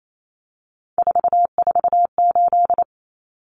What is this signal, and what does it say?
Morse code, heard as a keyed tone, about 28 wpm, 700 Hz. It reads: 448
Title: Numbers - 3-Digit 28wpm